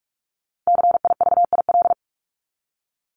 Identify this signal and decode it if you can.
Morse code, heard as a keyed tone, about 44 wpm, 700 Hz. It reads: CIVIL
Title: Top 1751-2000 Words 45wpm